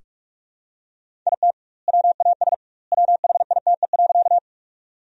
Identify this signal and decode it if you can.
Morse code, heard as a keyed tone, about 42 wpm, 700 Hz. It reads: IT WAS WHITE.
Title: 2x Word-Spacing - Sentences from Top 300 Words - 45wpm